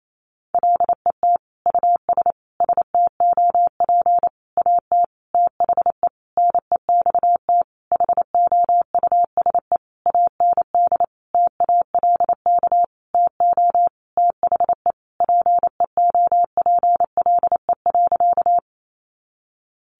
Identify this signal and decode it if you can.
Morse code, heard as a keyed tone, about 28 wpm, 700 Hz. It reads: LET US STOP AT THE NEXT HOUSE AND TALK TO THE PEOPLE.
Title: Sentences from Top 400 Words 28wpm